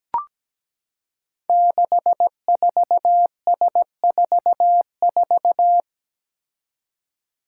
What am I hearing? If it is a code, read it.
Morse code, heard as a keyed tone, about 17 wpm, 700 Hz. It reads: 64S44
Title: Five Challenging Letter-Number Combinations 17wpm